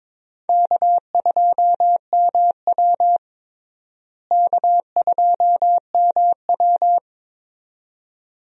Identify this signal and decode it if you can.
Morse code, heard as a keyed tone, about 21 wpm, 700 Hz. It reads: K2MW K2MW
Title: USA Call Signs - 4 Characters - Repeated 2x - 22wpm